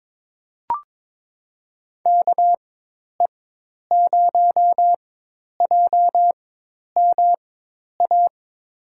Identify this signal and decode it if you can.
Morse code, heard as a keyed tone, about 22 wpm, spaced as for 10 wpm, 700 Hz. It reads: KE0JMA
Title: USA Call Signs - 6 Characters - 4x Character Spacing - 22wpm